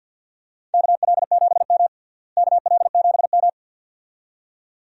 Morse code, heard as a keyed tone, about 53 wpm, 700 Hz. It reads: KL7M KL7M
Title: DXCC-100 Call Signs - 4 Characters - Repeated 2x - 50wpm